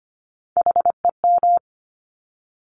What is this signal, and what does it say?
Morse code, heard as a keyed tone, about 25 wpm, 700 Hz. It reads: HEM
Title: Top Three Letter Combinations in Words 25wpm